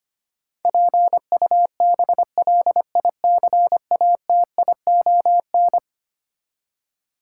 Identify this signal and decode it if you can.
Morse code, heard as a keyed tone, about 25 wpm, 700 Hz. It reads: PUBLICATION
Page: morse-code-ninja-practice-25wpm